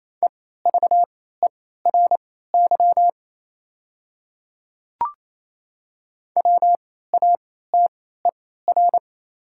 Morse code, heard as a keyed tone, about 28 wpm, spaced as for 16 wpm, 700 Hz. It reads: EVERY WATER
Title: 3x Character Spacing - Top 200 Words - 28wpm